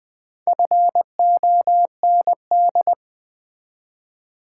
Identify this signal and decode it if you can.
Morse code, heard as a keyed tone, about 20 wpm, 700 Hz. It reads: FOND
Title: Build A Word - 1 Letter At A Time - 3 to 5-Letter Words - 20wpm